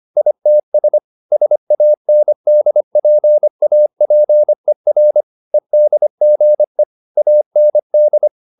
Morse code, heard as a keyed tone, about 25 wpm, 600 Hz. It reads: ITS SANDPAPER EDGE AND
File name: the old man and the sea part 1 25wpm